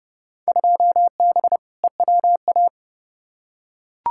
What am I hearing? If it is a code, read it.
Morse code, heard as a keyed tone, about 29 wpm, 700 Hz. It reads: 2B EWA
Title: Field Day Exchanges Encore - 30wpm